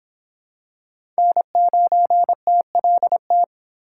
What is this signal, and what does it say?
Morse code, heard as a keyed tone, about 25 wpm, 700 Hz. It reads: N9TLT